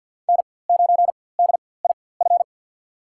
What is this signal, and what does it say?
Morse code, heard as a keyed tone, about 46 wpm, spaced as for 23 wpm, 700 Hz. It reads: N9DIF
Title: USA Call Signs - 5 Characters - 4x Character Spacing - 50wpm